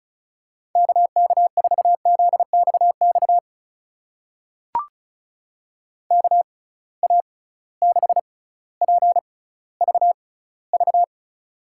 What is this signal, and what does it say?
Morse code, heard as a keyed tone, about 36 wpm, spaced as for 18 wpm, 700 Hz. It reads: KK4ZXX KA6PVV
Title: USA Call Signs - 6 Characters - 6x Character Spacing - 35wpm